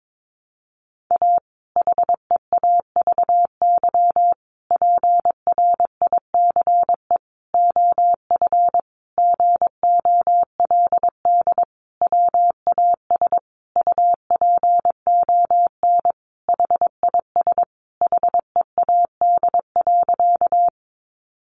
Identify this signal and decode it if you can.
Morse code, heard as a keyed tone, about 22 wpm, 700 Hz. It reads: A HEAVY PRICE OF GOLD WAS UPON HIS HEAD.
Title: Sentences from Top 1100 Words 22wpm